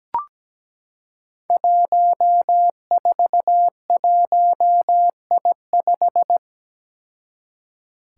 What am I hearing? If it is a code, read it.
Morse code, heard as a keyed tone, about 17 wpm, 700 Hz. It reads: 141I5
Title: Five Challenging Letter-Number Combinations 17wpm